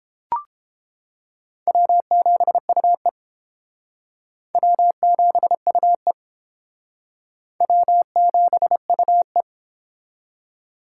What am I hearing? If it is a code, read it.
Morse code, heard as a keyed tone, about 31 wpm, 700 Hz. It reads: W7UE W7UE W7UE